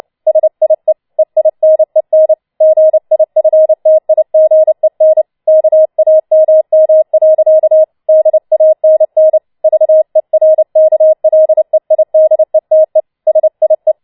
Morse code, heard as a keyed tone, about 29 wpm, 630 Hz. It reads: SIE EINEN GIFTIGEN KAMM. DANN VERKLEIDETE SIE
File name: Schneewittchen Teil3 30wpm